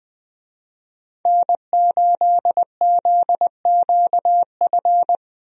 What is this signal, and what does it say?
Morse code, heard as a keyed tone, about 21 wpm, 700 Hz. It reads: N8ZQF
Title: USA Call Signs - 5 Characters - Repeated 3x - 20wpm